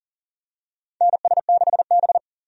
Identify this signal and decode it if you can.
Morse code, heard as a keyed tone, about 40 wpm, 700 Hz. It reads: NS6B
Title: USA Call Signs - 4 Characters - 40wpm